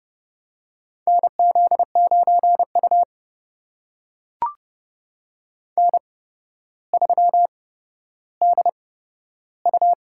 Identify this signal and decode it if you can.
Morse code, heard as a keyed tone, about 31 wpm, spaced as for 13 wpm, 700 Hz. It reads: NZ9U N3DU
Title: USA Call Signs - 4 Characters - 8x Character Spacing - 30wpm